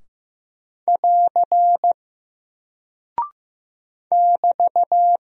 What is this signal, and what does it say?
Morse code, heard as a keyed tone, about 15 wpm, 700 Hz. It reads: <AR> <BT>
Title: Single Character (Letter:Number:Punctuation:Prosign) Alt 15wpm